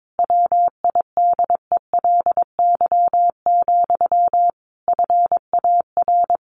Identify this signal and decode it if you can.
Morse code, heard as a keyed tone, about 22 wpm, 700 Hz. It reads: WIDELY, FAR
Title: Sentences from Top 1200 Words 22wpm